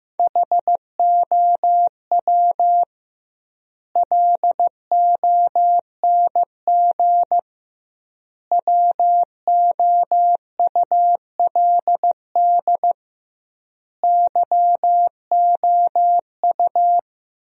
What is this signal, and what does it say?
Morse code, heard as a keyed tone, about 15 wpm, 700 Hz. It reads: HOW LONG WOULD YOU
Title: 2x Word-Spacing - Sentences from Top 500 Words - 15wpm